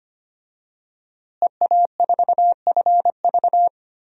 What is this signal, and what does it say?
Morse code, heard as a keyed tone, about 25 wpm, 700 Hz. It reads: EA4FV
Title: DXCC-100 Call Signs - 5 Characters - Repeated 2x - 25wpm